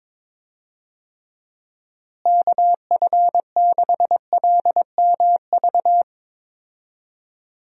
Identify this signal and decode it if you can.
Morse code, heard as a keyed tone, about 22 wpm, 700 Hz. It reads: KF6LMV
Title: USA Call Signs - 6 Characters - Repeated 3x - 22wpm